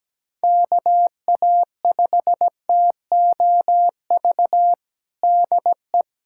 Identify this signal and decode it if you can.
Morse code, heard as a keyed tone, about 17 wpm, 700 Hz. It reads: KA5TOV DE
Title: USA Callsign DE Callsign - 6 Characters - 17wpm